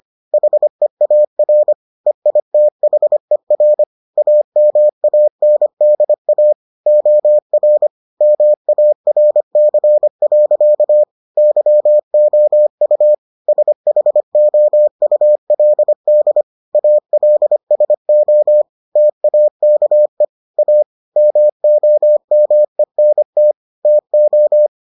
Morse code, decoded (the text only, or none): HEAR EITHER AMANDA OR MARC. YOU SHOULD ALSO TAKE A MOMENT TO